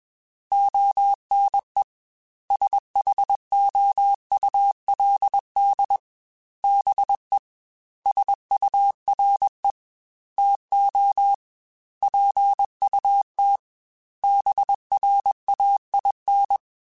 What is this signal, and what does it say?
Morse code, heard as a keyed tone, about 21 wpm, 800 Hz. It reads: ONE SHOULD BE SURE TO PUT BRAIN
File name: BrainMouth @20